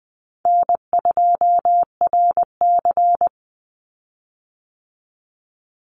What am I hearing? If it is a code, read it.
Morse code, heard as a keyed tone, about 20 wpm, 700 Hz. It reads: N2RC
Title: Easy Call Signs Repeated 3x - 20wpm